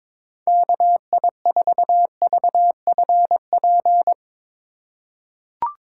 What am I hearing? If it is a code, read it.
Morse code, heard as a keyed tone, about 22 wpm, 700 Hz. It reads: KI4VFP
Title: USA Call Signs - 6 Characters - 4x Character Spacing - 22wpm